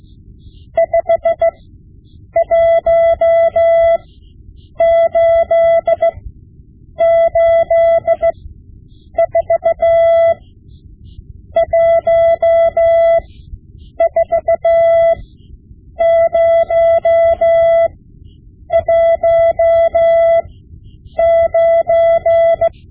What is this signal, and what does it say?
Morse code, heard as a keyed tone, about 16 wpm, spaced as for 8 wpm, 670 Hz. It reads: 51884 14019
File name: Morse Lessons / Morse Lessons / Tape-5B